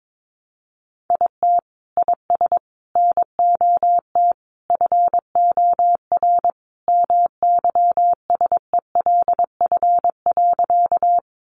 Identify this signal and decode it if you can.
Morse code, heard as a keyed tone, about 22 wpm, 700 Hz. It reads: IT IS NOT FOR MYSELF.
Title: Sentences from Top 1300 Words 22wpm